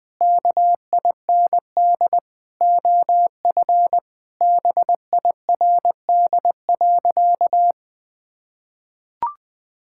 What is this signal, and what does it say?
Morse code, heard as a keyed tone, about 21 wpm, 700 Hz. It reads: KIND OF BIRD.
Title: Sentences with 3-Word Phrases - Spoken Beginning - 20wpm